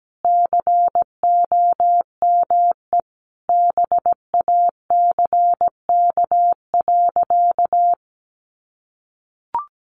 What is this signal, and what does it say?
Morse code, heard as a keyed tone, about 17 wpm, 700 Hz. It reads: COME BACK.
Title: Sentences from Top 1500 Words 17wpm